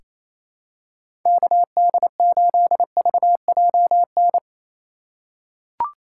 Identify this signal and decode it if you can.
Morse code, heard as a keyed tone, about 28 wpm, 700 Hz. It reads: KD8VJN
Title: USA Call Signs - 6 Characters - 3x Character Spacing - 28wpm